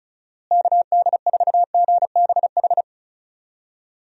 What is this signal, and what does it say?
Morse code, heard as a keyed tone, about 36 wpm, 700 Hz. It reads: KD4GBH
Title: USA Call Signs - 6 Characters - Repeated 2x - 35wpm